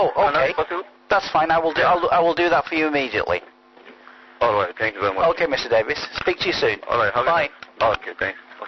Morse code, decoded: none